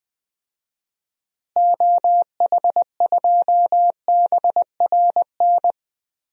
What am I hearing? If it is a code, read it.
Morse code, heard as a keyed tone, about 20 wpm, 700 Hz. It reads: OH2BRN
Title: DXCC-100 Call Signs - 6 Characters - Repeated 3x - 20wpm